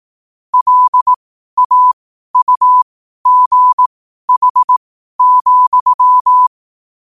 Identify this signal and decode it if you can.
Morse code, heard as a keyed tone, about 18 wpm, spaced as for 12 wpm, 1000 Hz. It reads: LAUGH,
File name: qotdfiles / 12wpm